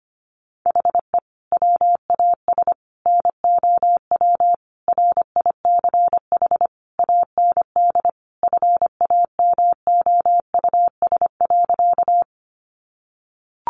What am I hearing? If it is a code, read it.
Morse code, heard as a keyed tone, about 25 wpm, 700 Hz. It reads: HE WAS NOW RICH AND FAMOUS.
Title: Sentences from Top 1500 Words 25wpm